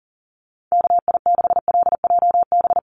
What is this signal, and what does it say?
Morse code, heard as a keyed tone, about 40 wpm, 700 Hz. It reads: KI6LJB
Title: USA Call Signs 40wpm